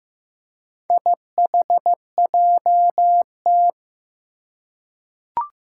Morse code, heard as a keyed tone, about 15 wpm, 700 Hz. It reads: IHJT
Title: Four Challenging Letter-Number Combinations 15wpm